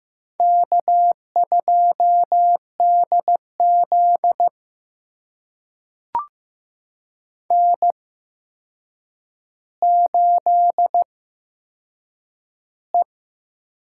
Morse code, heard as a keyed tone, about 15 wpm, spaced as for 7 wpm, 700 Hz. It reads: K2DZ N8E
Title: USA Call Signs - 4 Characters - 8x Character Spacing - 15wpm